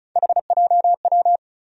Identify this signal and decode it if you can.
Morse code, heard as a keyed tone, about 36 wpm, 700 Hz. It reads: HJW
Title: USA Call Sign Postfixes - 3 Letters 35wpm